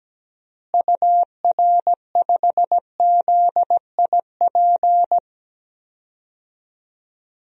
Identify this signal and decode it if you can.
Morse code, heard as a keyed tone, about 17 wpm, 700 Hz. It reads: UR5ZIP
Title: DXCC-100 Call Signs - 6 Characters - Repeated 3x - 17wpm